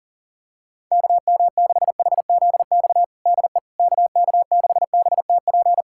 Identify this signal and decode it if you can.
Morse code, heard as a keyed tone, about 40 wpm, 700 Hz. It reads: KM6HZX DE KK6BTP
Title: USA Callsign DE Callsign 40wpm